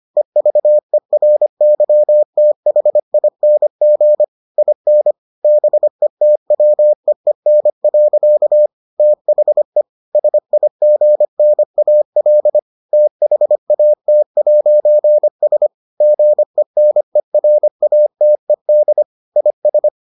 Morse code, heard as a keyed tone, about 25 wpm, 600 Hz. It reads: EVERYTHING IN BETWEEN. THE SIGNAL THAT'S GENERATED IS